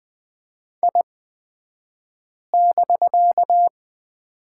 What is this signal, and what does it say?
Morse code, heard as a keyed tone, about 20 wpm, spaced as for 5 wpm, 700 Hz. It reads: I<BK>
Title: 249 Lesson 40d - BK - Two Characters 20wpm